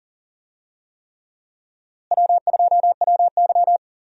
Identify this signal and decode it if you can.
Morse code, heard as a keyed tone, about 37 wpm, 700 Hz. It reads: W2WY
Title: USA Call Signs - 4 Characters - Repeated 3x - 40wpm